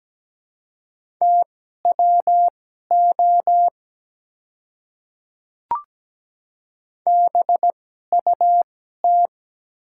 Morse code, heard as a keyed tone, about 17 wpm, spaced as for 12 wpm, 700 Hz. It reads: TWO BUT
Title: 2x Character Spacing - Top 100 Words - 17wpm